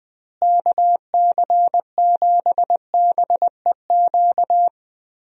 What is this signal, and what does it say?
Morse code, heard as a keyed tone, about 20 wpm, 700 Hz. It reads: KC7BEQ
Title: USA Call Signs - 6 Characters - Repeated 2x - 20wpm